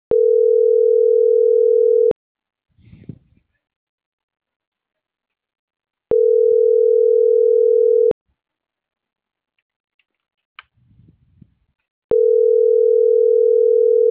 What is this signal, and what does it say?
Ringback tone (North America), 440 + 480 Hz.